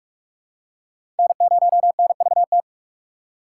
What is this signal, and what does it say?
Morse code, heard as a keyed tone, about 43 wpm, 700 Hz. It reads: N0NVT
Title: USA Call Signs - 5 Characters - 4x Character Spacing - 45wpm